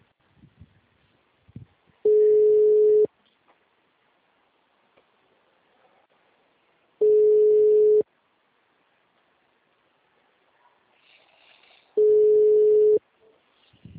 Ringback tone (Europe, 425 Hz).